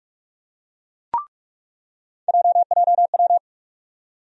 Morse code, heard as a keyed tone, about 44 wpm, 700 Hz. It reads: JJW
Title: USA Call Sign Postfixes - 3 Letters 45wpm